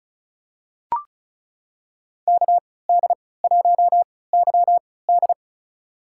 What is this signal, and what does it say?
Morse code, heard as a keyed tone, about 36 wpm, spaced as for 20 wpm, 700 Hz. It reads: KD1YD
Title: USA Call Signs - 5 Characters - 3x Character Spacing - 35wpm